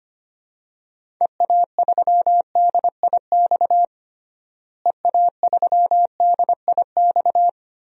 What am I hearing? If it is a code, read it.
Morse code, heard as a keyed tone, about 25 wpm, 700 Hz. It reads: EA3DIX EA3DIX
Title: DXCC-100 Call Signs - 6 Characters - Repeated 3x - 25wpm